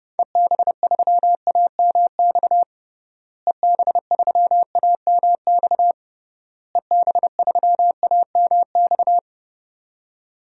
Morse code, heard as a keyed tone, about 32 wpm, 700 Hz. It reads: EB3AMX EB3AMX EB3AMX
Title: DXCC-100 Call Signs - 6 Characters - Repeated 3x - 30wpm